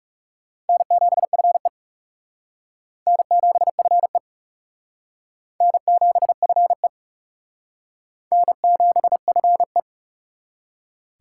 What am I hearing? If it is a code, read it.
Morse code, heard as a keyed tone, about 37 wpm, 700 Hz. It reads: N7FE N7FE N7FE N7FE
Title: Speed Racing - USA 1x2 Call Signs 30wpm